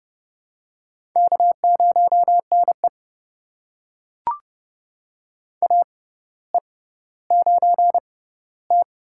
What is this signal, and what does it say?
Morse code, heard as a keyed tone, about 30 wpm, spaced as for 15 wpm, 700 Hz. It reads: K0NE AE9T
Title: USA Call Signs - 6x Character Spacing - 30wpm